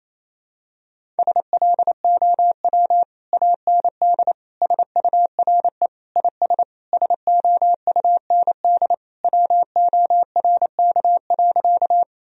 Morse code, heard as a keyed tone, about 28 wpm, 700 Hz. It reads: SLOW AND SURE IS SOUND WORK.
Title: Sentences from Top 1000 Words - Review Part 2 - 28wpm